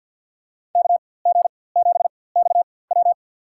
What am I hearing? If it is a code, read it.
Morse code, heard as a keyed tone, about 53 wpm, spaced as for 24 wpm, 700 Hz. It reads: KG7XW
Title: USA Call Signs - 5 Characters - 4x Character Spacing - 50wpm